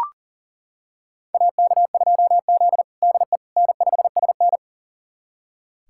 Morse code, heard as a keyed tone, about 40 wpm, 700 Hz. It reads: AK2Z DE N5SN
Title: USA Callsign DE Callsign - 4 Characters - 40wpm